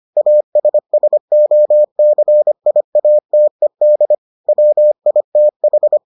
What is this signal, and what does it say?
Morse code, heard as a keyed tone, about 25 wpm, 600 Hz. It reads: ASSOCIATED WITH